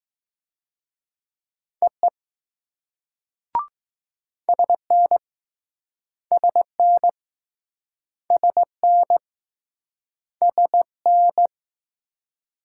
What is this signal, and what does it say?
Morse code, heard as a keyed tone, about 20 wpm, 700 Hz. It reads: EE SN SN SN SN